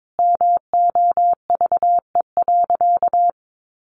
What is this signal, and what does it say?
Morse code, heard as a keyed tone, about 22 wpm, 700 Hz. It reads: MOVE.
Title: Sentences from Top 600 Words 22wpm